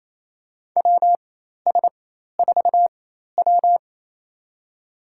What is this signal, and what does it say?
Morse code, heard as a keyed tone, about 28 wpm, spaced as for 13 wpm, 700 Hz. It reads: WS4W
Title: USA Call Signs - 4 Characters - 4x Character Spacing - 28wpm